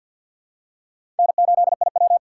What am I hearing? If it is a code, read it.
Morse code, heard as a keyed tone, about 51 wpm, 700 Hz. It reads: N8IW
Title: USA Call Signs - 4 Characters - 3x Character Spacing - 50wpm